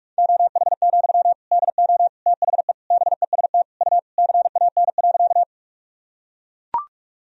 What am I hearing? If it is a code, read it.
Morse code, heard as a keyed tone, about 44 wpm, 700 Hz. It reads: OH, DO THE BEST U CAN.
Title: Sentences from Top 500 Words and Abbreviations 45wpm